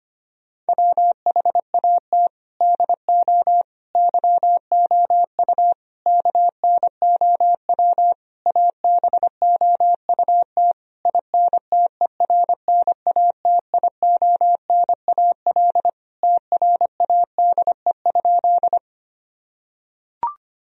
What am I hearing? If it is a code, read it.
Morse code, heard as a keyed tone, about 25 wpm, 700 Hz. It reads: WHAT DO YOU KNOW ABOUT INTERNATIONAL TRADE?
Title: Sentences from Top 1000 Words - Review Part 4 - 25wpm